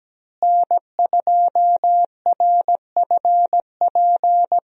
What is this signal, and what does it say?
Morse code, heard as a keyed tone, about 17 wpm, 700 Hz. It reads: N2RFP
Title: USA Call Signs - 5 Characters - Repeated 3x - 17wpm